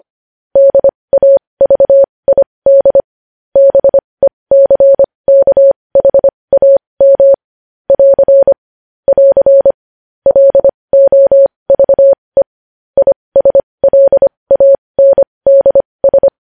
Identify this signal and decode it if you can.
Morse code, heard as a keyed tone, about 25 wpm, spaced as for 20 wpm, 550 Hz. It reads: DAVID BECKHAM <AR> <AR> LOVE ISLANDS
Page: News Headlines in Morse Code at 20 WPM